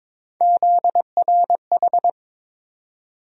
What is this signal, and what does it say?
Morse code, heard as a keyed tone, about 22 wpm, 700 Hz. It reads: ZRH
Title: USA Call Sign Postfixes - 3 Letters 22wpm